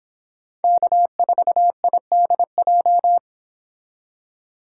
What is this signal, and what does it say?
Morse code, heard as a keyed tone, about 25 wpm, 700 Hz. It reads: K4IDJ